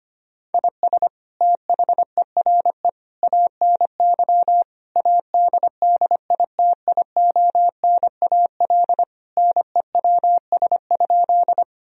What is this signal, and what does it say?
Morse code, heard as a keyed tone, about 25 wpm, 700 Hz. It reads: IS THERE ANY ADDITIONAL NEWS?
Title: Sentences from Top 1000 Words - Review Part 4 - 25wpm